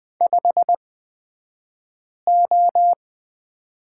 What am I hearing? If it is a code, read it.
Morse code, heard as a keyed tone, about 20 wpm, spaced as for 5 wpm, 700 Hz. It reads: 5O
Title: 223 Lesson 36d - BT - Two Characters 20wpm